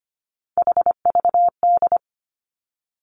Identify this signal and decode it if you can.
Morse code, heard as a keyed tone, about 25 wpm, 700 Hz. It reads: HVD
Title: Three Challenging Letter-Number Combinations 25wpm